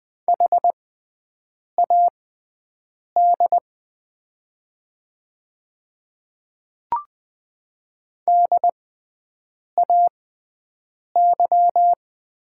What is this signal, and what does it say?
Morse code, heard as a keyed tone, about 20 wpm, spaced as for 7 wpm, 700 Hz. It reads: HAD DAY
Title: 6x Character Spacing - Top 200 Words - 20wpm